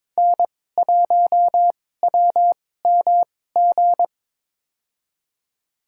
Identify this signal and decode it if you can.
Morse code, heard as a keyed tone, about 22 wpm, spaced as for 16 wpm, 690 Hz. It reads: N1WMG
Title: USA Call Signs - 5 Characters - 2x Character Spacing - 22wpm